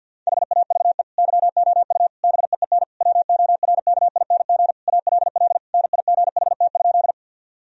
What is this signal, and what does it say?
Morse code, heard as a keyed tone, about 50 wpm, 700 Hz. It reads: HAVE YOU BEEN WORKING ALL NIGHT?
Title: Sentences from Top 1000 Words - Review Part 4 - 50wpm